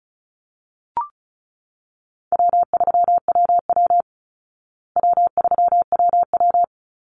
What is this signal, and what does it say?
Morse code, heard as a keyed tone, about 34 wpm, 700 Hz. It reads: W3WW W3WW
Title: Call Signs Repeated 2x - 35wpm